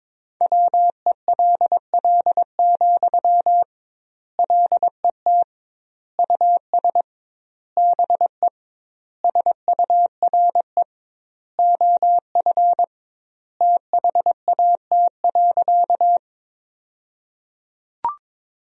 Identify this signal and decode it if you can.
Morse code, heard as a keyed tone, about 22 wpm, 700 Hz. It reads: WELL, LET US BE SURE OF THAT.
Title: 2x Word-Spacing - Sentences from Top 300 Words - 22wpm